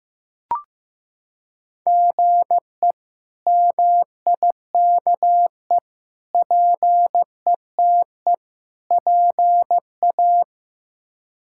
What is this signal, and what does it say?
Morse code, heard as a keyed tone, about 15 wpm, 700 Hz. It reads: GE MIKE PETE PA
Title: K1USN SST QSO Practice - 15wpm